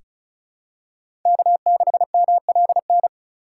Morse code, heard as a keyed tone, about 36 wpm, 700 Hz. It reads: K6MLN
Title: USA Call Signs - 4x Character Spacing - 35wpm